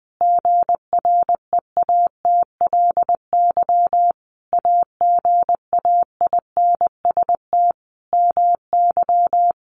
Morse code, heard as a keyed tone, about 20 wpm, 700 Hz. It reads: GREATLY AGAINST MY
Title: Sentences from Top 1400 Words 20wpm